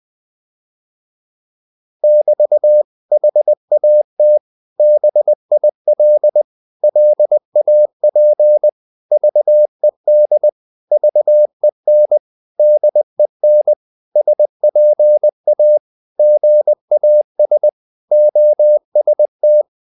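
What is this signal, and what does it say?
Morse code, heard as a keyed tone, about 20 wpm, 600 Hz. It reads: <BT> HAT BIL LAP VED VEN DEN SPA GAS OST
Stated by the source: Morse code practice of short Danish words at 20 wpm, from CW med Gnister.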